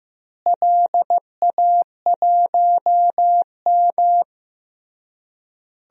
Morse code, heard as a keyed tone, about 15 wpm, 700 Hz. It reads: LA1M
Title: DXCC-100 Call Signs - 4 Characters - Repeated 3x - 15wpm